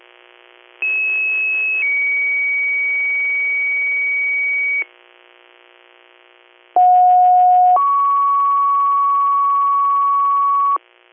Two-tone page, heard as a one-second tone, then a three-second tone, about 2600 then 2250 Hz, about 730 then 1120 Hz. It is heard from 0.8 to 10.8 s.